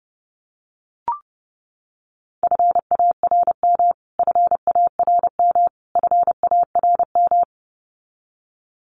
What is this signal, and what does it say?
Morse code, heard as a keyed tone, about 28 wpm, 700 Hz. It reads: FARM FARM FARM
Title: Top 500 Words Repeated 30wpm